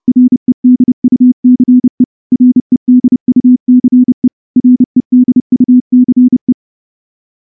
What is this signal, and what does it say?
Morse code, heard as a keyed tone, about 28 wpm, 250 Hz. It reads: REDUCE REDUCE REDUCE
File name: morse-training-30WPM